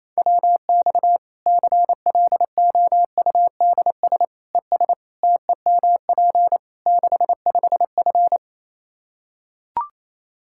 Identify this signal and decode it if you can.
Morse code, heard as a keyed tone, about 28 wpm, 700 Hz. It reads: WX CLOUDS ES TEMP 65F
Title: Practice QSO - Exchange 2 - Info - 28wpm